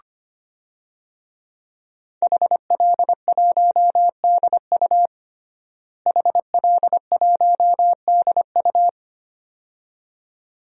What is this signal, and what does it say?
Morse code, heard as a keyed tone, about 25 wpm, 700 Hz. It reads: HL1DU HL1DU
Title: DXCC-100 Call Signs - 5 Characters - Repeated 2x - 25wpm